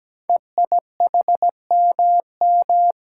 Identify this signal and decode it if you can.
Morse code, heard as a keyed tone, about 17 wpm, 700 Hz. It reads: EIHMM
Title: Code Groups - 5 Letters - 17wpm